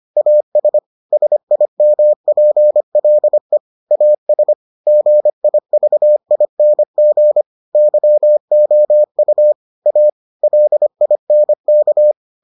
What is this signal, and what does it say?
Morse code, heard as a keyed tone, about 25 wpm, 600 Hz. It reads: AS SIMPLE AS GIVING YOU A LINK